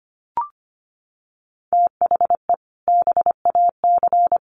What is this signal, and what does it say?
Morse code, heard as a keyed tone, about 25 wpm, 700 Hz. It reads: THE BAC
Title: Sets of 4 Words 25wpm